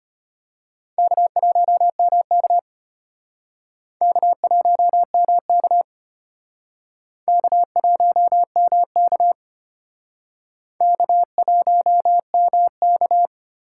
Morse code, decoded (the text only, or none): K1MK K1MK K1MK K1MK